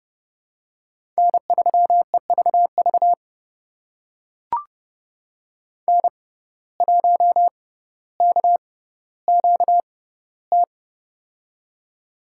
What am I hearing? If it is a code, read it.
Morse code, heard as a keyed tone, about 30 wpm, spaced as for 15 wpm, 700 Hz. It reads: N3EVV N1KQT
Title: USA Call Signs - 5 Characters - 6x Character Spacing - 30wpm